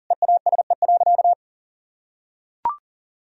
Morse code, heard as a keyed tone, about 40 wpm, 700 Hz. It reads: EASE.
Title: Sentences from Top 1000 Words - Review Part 1 - 40wpm